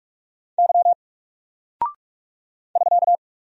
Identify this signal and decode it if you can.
Morse code, heard as a keyed tone, about 46 wpm, 700 Hz. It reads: Y <SK>
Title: Single Character (Letter:Number:Punctuation:Prosign) Alt 45wpm